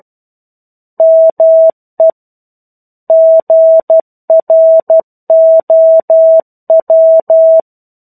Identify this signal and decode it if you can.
Morse code, heard as a keyed tone, about 12 wpm, 650 Hz. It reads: ME GROW
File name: 01-Chapter-01 12 WPM